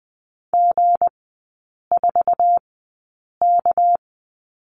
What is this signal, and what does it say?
Morse code, heard as a keyed tone, about 20 wpm, spaced as for 8 wpm, 690 Hz. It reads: G4K
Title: 198 Lesson 32e - Letter k - Three Characters 20wpm